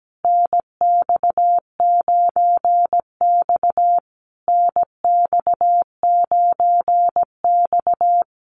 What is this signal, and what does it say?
Morse code, heard as a keyed tone, about 17 wpm, 700 Hz. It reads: NX9X NX9X
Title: Common QSO Phrases Encore 17wpm